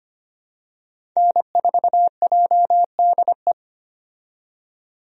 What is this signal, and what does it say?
Morse code, heard as a keyed tone, about 25 wpm, 700 Hz. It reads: N4JDE
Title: USA Call Signs - 5 Characters - 25wpm